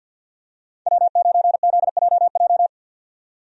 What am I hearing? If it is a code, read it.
Morse code, heard as a keyed tone, about 51 wpm, 700 Hz. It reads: W9ZJJ
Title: USA Call Signs - 5 Characters - 50wpm